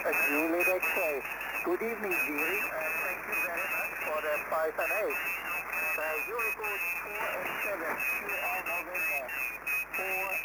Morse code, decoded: none